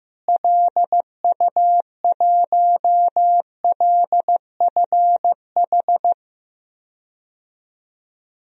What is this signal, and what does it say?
Morse code, heard as a keyed tone, about 15 wpm, 700 Hz. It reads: LU1LFH
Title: DXCC-100 Call Signs - 6 Characters - Repeated 3x - 15wpm